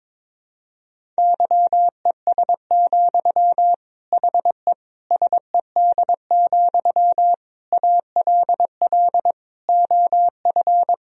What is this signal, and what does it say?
Morse code, heard as a keyed tone, about 22 wpm, 700 Hz. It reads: YES, HE SED, ALL OF
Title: Sentences from Top 500 Words and Abbreviations 22wpm